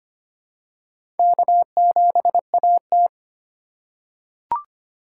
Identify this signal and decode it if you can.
Morse code, heard as a keyed tone, about 25 wpm, 700 Hz. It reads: K7AT
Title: USA Call Signs - 4 Characters - 3x Character Spacing - 25wpm